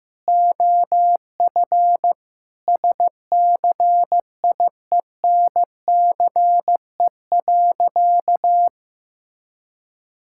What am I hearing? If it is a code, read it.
Morse code, heard as a keyed tone, about 15 wpm, 700 Hz. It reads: OF SCIENCE.
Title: Sentences from Top 1000 Words - Review Part 6 - 15wpm